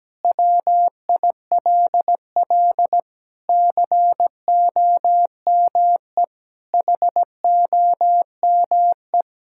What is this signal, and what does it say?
Morse code, heard as a keyed tone, about 17 wpm, 700 Hz. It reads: WILL COME HOME
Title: Sentences from Top 1000 Words - Review Part 5 - 17wpm